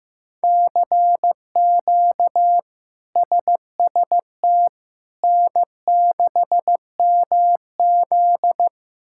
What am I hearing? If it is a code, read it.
Morse code, heard as a keyed tone, about 16 wpm, 700 Hz. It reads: CQ SST N6MZ
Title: K1USN SST QSO Practice - 15wpm